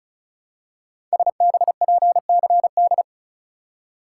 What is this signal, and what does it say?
Morse code, heard as a keyed tone, about 36 wpm, 700 Hz. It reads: SBPCD
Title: Code Groups - 5 Letters - 35wpm